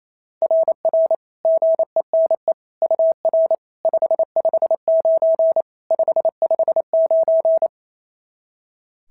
Morse code, heard as a keyed tone, about 27 wpm, 660 Hz. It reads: RR GENE UR 559 559
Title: SOTA QSO Practice 28wpm